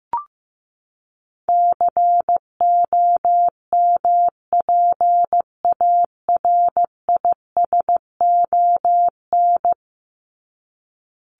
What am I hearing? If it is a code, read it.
Morse code, heard as a keyed tone, about 15 wpm, 700 Hz. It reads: COMPARISON
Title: Top 1501-1750 Words 15wpm